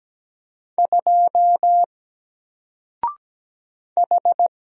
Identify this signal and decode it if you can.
Morse code, heard as a keyed tone, about 17 wpm, 700 Hz. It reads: 2 H
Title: Single Character (Letter:Number:Punctuation:Prosign) 17wpm